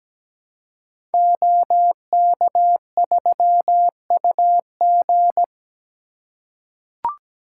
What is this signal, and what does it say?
Morse code, heard as a keyed tone, about 17 wpm, 700 Hz. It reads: OK3UG
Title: Code Groups - 5 Letter:Numbers - 17wpm